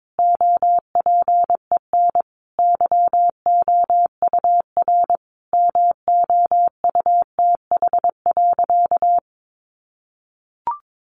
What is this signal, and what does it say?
Morse code, heard as a keyed tone, about 22 wpm, 700 Hz. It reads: OPEN YOUR MOUTH.
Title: Sentences from Top 1500 Words 22wpm